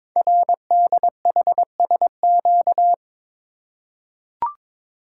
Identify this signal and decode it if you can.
Morse code, heard as a keyed tone, about 22 wpm, 700 Hz. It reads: RDHSQ
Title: Code Groups - 5 Letters - 22wpm